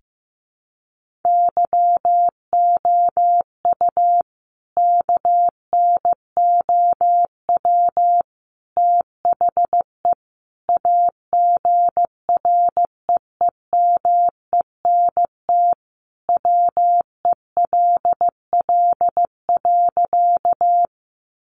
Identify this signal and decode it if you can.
Morse code, heard as a keyed tone, about 15 wpm, 700 Hz. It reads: YOU KNOW THE AGREEMENT WELL.
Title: Sentences from Top 1500 Words 15wpm